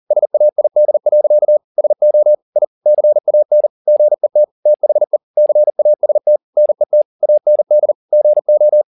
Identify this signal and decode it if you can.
Morse code, heard as a keyed tone, about 40 wpm, 600 Hz. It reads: SAID. SO I CAN GET THE CAST NET AND GO
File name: the old man and the sea 1 40wpm